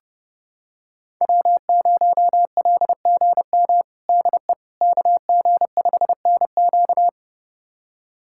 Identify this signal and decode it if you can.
Morse code, heard as a keyed tone, about 30 wpm, 700 Hz. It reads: W0LGM DE KG5NQ
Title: USA Callsign DE Callsign - 5 Characters - 30wpm